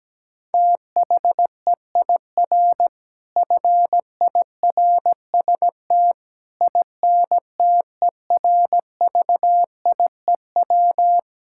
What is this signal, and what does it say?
Morse code, heard as a keyed tone, about 17 wpm, 700 Hz. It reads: THEIR FIRST INTERVIEW
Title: Sentences from Top 1600 Words 17wpm